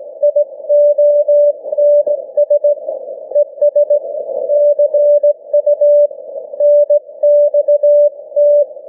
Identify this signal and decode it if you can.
Morse code, heard as a keyed tone, about 17 wpm, 590 Hz. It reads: IONS ES CU NXT
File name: cw / ssn 40M